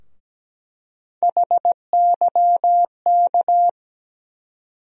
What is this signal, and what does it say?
Morse code, heard as a keyed tone, about 17 wpm, 700 Hz. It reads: HYK
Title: USA Call Sign Postfixes - 3 Letters 17wpm